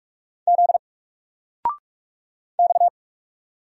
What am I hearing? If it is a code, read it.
Morse code, heard as a keyed tone, about 44 wpm, 700 Hz. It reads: Z X
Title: Single Letter-Number 45wpm